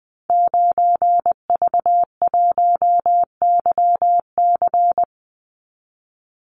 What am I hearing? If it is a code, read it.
Morse code, heard as a keyed tone, about 20 wpm, 700 Hz. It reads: 9V1YC
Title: Call Signs - 20wpm